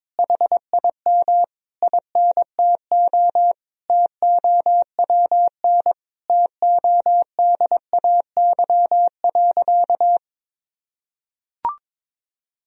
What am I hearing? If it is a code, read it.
Morse code, heard as a keyed tone, about 22 wpm, 700 Hz. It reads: HIM INTO TOWN TODAY.
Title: Sentences from Top 1000 Words - Review Part 3 - 22wpm